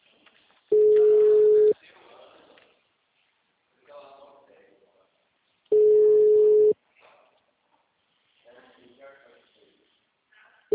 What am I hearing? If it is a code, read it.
Ringback tone (Europe, 425 Hz), with speech.